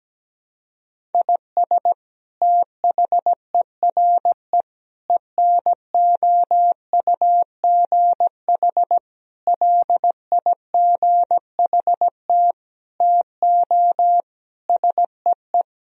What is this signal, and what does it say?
Morse code, heard as a keyed tone, about 17 wpm, 700 Hz. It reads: IS THERE ENOUGH LIGHT TO SEE
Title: Sentences from Top 1000 Words - Review Part 1 - 17wpm